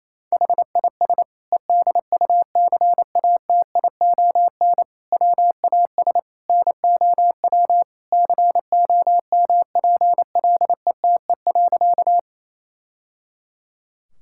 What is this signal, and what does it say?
Morse code, heard as a keyed tone, about 28 wpm, 700 Hz. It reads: HIS EDUCATION WAS NOW COMPLETE.
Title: Sentences from Top 1000 Words - Review Part 6 - 28wpm